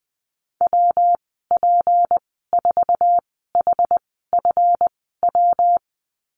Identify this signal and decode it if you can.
Morse code, heard as a keyed tone, about 20 wpm, spaced as for 15 wpm, 700 Hz. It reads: WP4HFW
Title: 179 Lesson 29g - Slash - Call Signs 20wpm